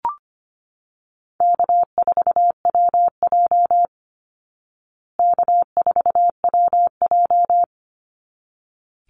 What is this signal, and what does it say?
Morse code, heard as a keyed tone, about 25 wpm, 700 Hz. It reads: K4WJ K4WJ
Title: Easy Call Signs Repeated 2x - 25wpm